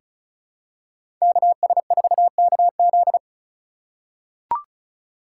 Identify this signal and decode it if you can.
Morse code, heard as a keyed tone, about 36 wpm, 700 Hz. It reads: KS4KZ
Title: USA Call Signs - 5 Characters - 6x Character Spacing - 35wpm